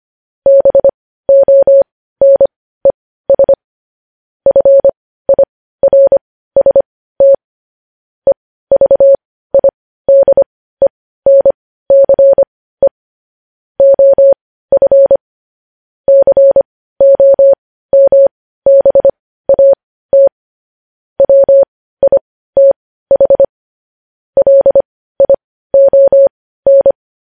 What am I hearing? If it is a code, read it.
Morse code, heard as a keyed tone, about 25 wpm, spaced as for 15 wpm, 550 Hz. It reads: BONES FIRST EVIDENCE OF COMBAT WITH LION
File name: podcast / News headlines-1x-25x15